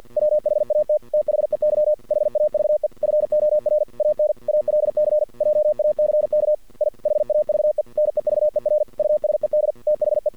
Morse code, heard as a keyed tone, about 50 wpm, 610 Hz. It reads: WHAT THEY HAVE FOR AN ALC JACK I HAVE NEVER USED THE
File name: archived / CRSnet